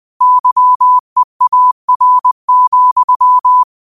Morse code, heard as a keyed tone, about 20 wpm, 1000 Hz. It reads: YEAR,